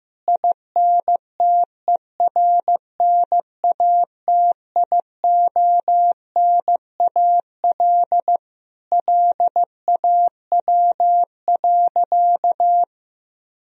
Morse code, heard as a keyed tone, about 14 wpm, 700 Hz. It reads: INTERNATIONAL LAW.
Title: Sentences from Top 1000 Words - Review Part 4 - 15wpm